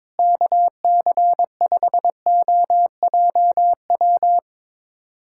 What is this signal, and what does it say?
Morse code, heard as a keyed tone, about 21 wpm, 700 Hz. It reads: KC5OJW
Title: USA Callsign DE Callsign 22wpm